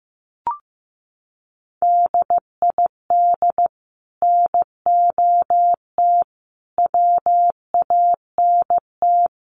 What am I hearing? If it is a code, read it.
Morse code, heard as a keyed tone, about 15 wpm, 700 Hz. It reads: DID NOT WANT
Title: Sets of 5 Words 15wpm